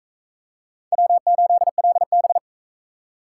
Morse code, heard as a keyed tone, about 39 wpm, 700 Hz. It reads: W8LB